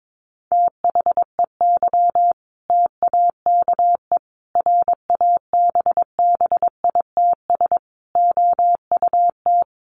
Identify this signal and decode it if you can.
Morse code, heard as a keyed tone, about 22 wpm, 700 Hz. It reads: THEY TAKE RABBITS OUT
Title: Easy Phrases 22wpm